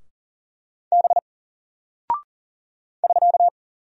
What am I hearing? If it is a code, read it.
Morse code, heard as a keyed tone, about 40 wpm, 700 Hz. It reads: B <SK>
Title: Single Character (Letter:Number:Punctuation:Prosign) Alt 40wpm